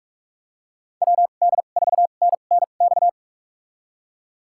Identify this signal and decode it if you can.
Morse code, heard as a keyed tone, about 46 wpm, spaced as for 33 wpm, 700 Hz. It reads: WD4NNX
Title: USA Call Signs - 6 Characters - 2x Character Spacing - 45wpm